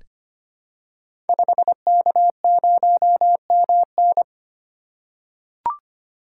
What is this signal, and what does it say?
Morse code, heard as a keyed tone, about 25 wpm, 700 Hz. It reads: 5K0MN
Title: Code Groups - 5 Letter:Numbers - 25wpm